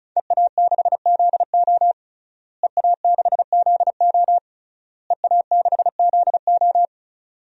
Morse code, heard as a keyed tone, about 36 wpm, 700 Hz. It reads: EA6ZO EA6ZO EA6ZO
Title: DXCC-100 Call Signs - 5 Characters - Repeated 3x - 35wpm